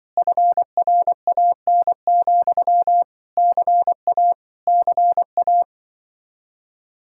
Morse code, heard as a keyed tone, about 25 wpm, 700 Hz. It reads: FRAN, CA CA
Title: Speed-Racing Common QSO Phrases - 20wpm